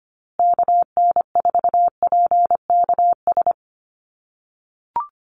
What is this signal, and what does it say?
Morse code, heard as a keyed tone, about 25 wpm, 700 Hz. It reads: KN4PKS
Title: USA Call Signs Repeated 2x 25wpm